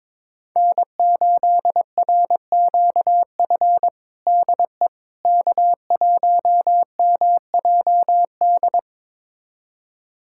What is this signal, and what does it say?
Morse code, heard as a keyed tone, about 22 wpm, 700 Hz. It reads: N8RQF DE K1MJD
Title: USA Callsign DE Callsign - 5 Characters - 22wpm